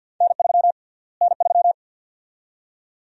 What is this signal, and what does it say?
Morse code, heard as a keyed tone, about 47 wpm, 700 Hz. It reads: N3 N3
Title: USA Call Sign Prefixes - Repeated 2x - 50wpm